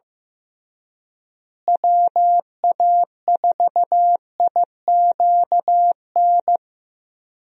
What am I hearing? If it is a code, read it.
Morse code, heard as a keyed tone, about 15 wpm, 700 Hz. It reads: WA4IQN
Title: USA Call Signs - 6 Characters - Repeated 3x - 15wpm